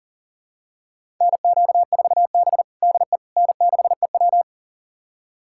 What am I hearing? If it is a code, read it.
Morse code, heard as a keyed tone, about 40 wpm, 700 Hz. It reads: NQ4B DE N6EW
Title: USA Callsign DE Callsign - 4 Characters - 40wpm